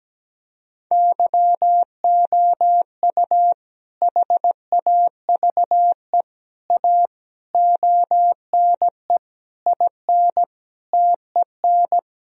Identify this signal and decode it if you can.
Morse code, heard as a keyed tone, about 17 wpm, 700 Hz. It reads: YOU HAVE A ONE IN TEN
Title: Sentences from Top 1000 Words - Review Part 2 - 17wpm